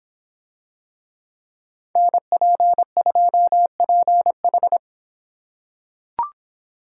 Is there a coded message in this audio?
Yes: Morse code from 2.0 to 4.8 s, heard as a keyed tone, about 25 wpm, 700 Hz. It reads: NP2PH